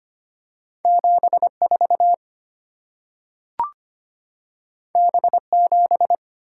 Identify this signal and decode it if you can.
Morse code, heard as a keyed tone, about 25 wpm, 700 Hz. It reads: 74 B7
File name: Two Letter-Number Combinations - 25wpm